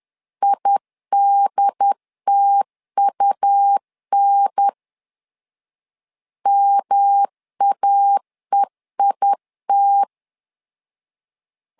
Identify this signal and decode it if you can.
Morse code, heard as a keyed tone, about 11 wpm, 800 Hz. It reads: IDTUN MAEIT
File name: LEZIONI / 3 MWRO / 3 M 10wpm 1828